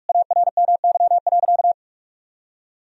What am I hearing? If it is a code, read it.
Morse code, heard as a keyed tone, about 42 wpm, 700 Hz. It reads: ARMY.
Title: Sentences from Top 1000 Words - Review Part 1 - 45wpm